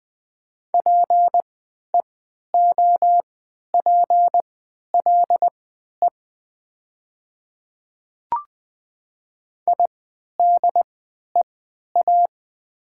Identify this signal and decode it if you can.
Morse code, heard as a keyed tone, about 20 wpm, spaced as for 11 wpm, 700 Hz. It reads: PEOPLE IDEA
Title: 3x Character Spacing - Top 300 Words - 20wpm